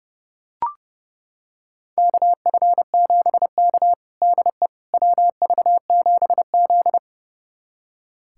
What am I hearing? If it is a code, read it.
Morse code, heard as a keyed tone, about 30 wpm, 700 Hz. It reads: KF7K DE WV7Z
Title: USA Callsign DE Callsign - 4 Characters - 30wpm